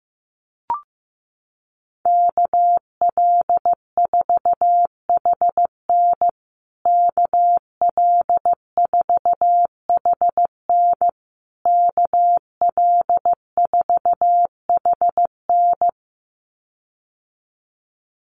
Morse code, heard as a keyed tone, about 15 wpm, 700 Hz. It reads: KL4HN KL4HN KL4HN
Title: USA Call Signs Repeated 3x 15wpm